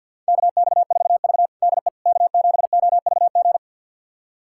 Morse code, heard as a keyed tone, about 48 wpm, 690 Hz. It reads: KX4V DE K7OVG
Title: USA Callsign DE Callsign 50wpm